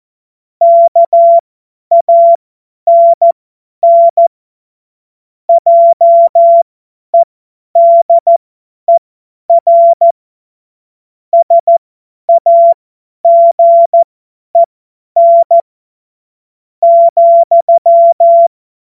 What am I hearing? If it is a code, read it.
Morse code, heard as a keyed tone, about 14 wpm, spaced as for 9 wpm, 680 Hz. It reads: KANN JEDER SAGEN,
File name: Morsekurs / Witze / Witz 20. Januar 2026 50 F70